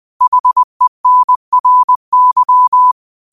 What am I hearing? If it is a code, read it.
Morse code, heard as a keyed tone, about 20 wpm, 1000 Hz. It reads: HENRY
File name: qotdfiles / 20wpm